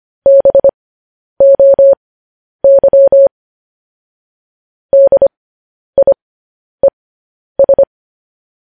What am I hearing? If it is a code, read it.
Morse code, heard as a keyed tone, about 25 wpm, spaced as for 10 wpm, 550 Hz. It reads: BOY DIES
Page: news headlines in morse code at 10 wpm